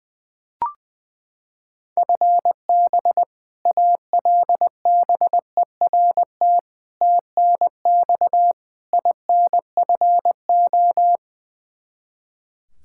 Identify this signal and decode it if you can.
Morse code, heard as a keyed tone, about 21 wpm, 700 Hz. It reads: FB ALBERT TNX INFO
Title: Practice QSO - Exchange 3 - More Info - 20wpm